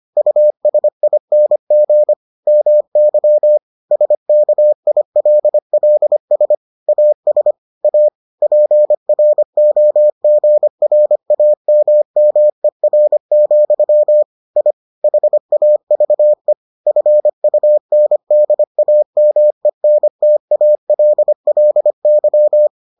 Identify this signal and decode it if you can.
Morse code, heard as a keyed tone, about 25 wpm, 600 Hz. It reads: USING MY SKILLS AS A PROGRAMMER, I HAVE FUNDAMENTALLY